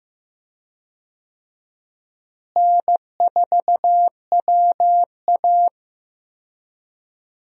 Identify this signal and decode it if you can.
Morse code, heard as a keyed tone, about 15 wpm, 700 Hz. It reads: N4WA
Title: USA Call Signs - 4 Characters - Repeated 3x - 15wpm